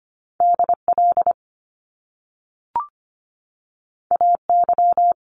Morse code, heard as a keyed tone, about 25 wpm, 700 Hz. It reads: DL AY
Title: Top Two Letter Combinations in Words 25wpm